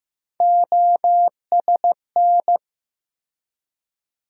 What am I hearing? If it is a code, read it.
Morse code, heard as a keyed tone, about 16 wpm, 700 Hz. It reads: OSN
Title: USA Call Sign Postfixes - 3 Letters 15wpm